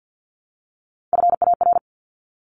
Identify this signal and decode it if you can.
Morse code, heard as a keyed tone, about 50 wpm, 700 Hz. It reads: FAR
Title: Top 500 Words 50wpm